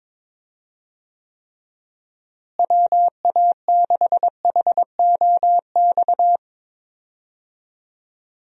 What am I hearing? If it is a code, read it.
Morse code, heard as a keyed tone, about 22 wpm, 700 Hz. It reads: WA6HOX
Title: USA Call Signs - 6 Characters - Repeated 2x - 22wpm